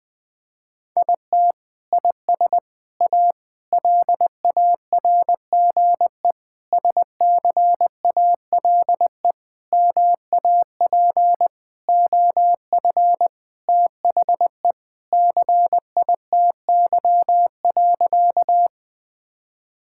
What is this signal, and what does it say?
Morse code, heard as a keyed tone, about 21 wpm, 700 Hz. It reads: IT IS A LARGE SCALE MAP OF THE CITY.
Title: Sentences from Top 1000 Words - Review Part 5 - 20wpm